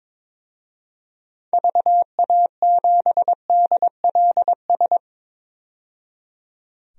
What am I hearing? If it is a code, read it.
Morse code, heard as a keyed tone, about 21 wpm, 700 Hz. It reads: VA7DLS
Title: DXCC-100 Call Signs - 6 Characters - 22wpm